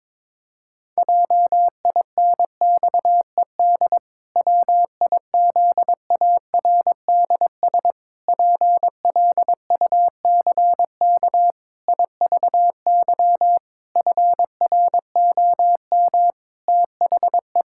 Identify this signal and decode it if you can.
Morse code, heard as a keyed tone, about 22 wpm, 700 Hz. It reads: JINXED WIZARDS PLUCK IVY FROM THE
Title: Pangram Challenge v2 - 22wpm